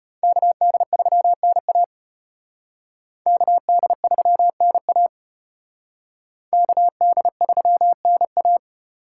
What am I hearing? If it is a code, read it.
Morse code, heard as a keyed tone, about 33 wpm, 700 Hz. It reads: KD3NA KD3NA KD3NA